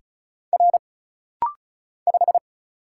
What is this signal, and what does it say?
Morse code, heard as a keyed tone, about 36 wpm, 700 Hz. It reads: R 5
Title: Single Letter-Number - Rapid-Fire 35wpm